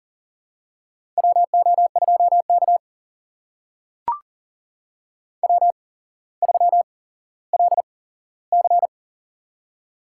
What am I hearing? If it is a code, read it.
Morse code, heard as a keyed tone, about 40 wpm, spaced as for 17 wpm, 700 Hz. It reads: WO2K W3LC
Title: USA Call Signs - 4 Characters - 8x Character Spacing - 40wpm